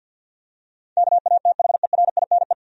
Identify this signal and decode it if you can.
Morse code, heard as a keyed tone, about 51 wpm, 700 Hz. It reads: KATHERINE
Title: USA Common Names - 50wpm